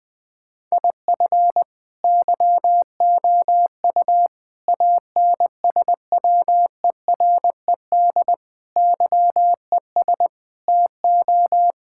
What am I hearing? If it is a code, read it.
Morse code, heard as a keyed tone, about 20 wpm, 700 Hz. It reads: IF YOU ANSWERED YES TO
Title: Sentences with 9-Word Phrases - 20wpm